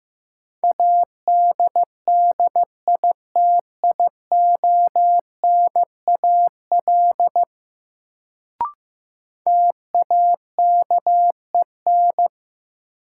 Morse code, heard as a keyed tone, about 15 wpm, 700 Hz. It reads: ADDITIONAL TAKEN
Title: Sentences from Top 1000 Words - Review Part 4 - New Word Drill - 15wpm